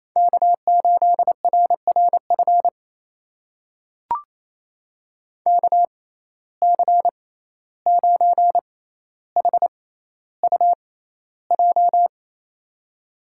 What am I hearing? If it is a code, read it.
Morse code, heard as a keyed tone, about 28 wpm, spaced as for 14 wpm, 700 Hz. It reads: K8RRF KC9HUJ
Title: USA Call Signs - 6x Character Spacing - 28wpm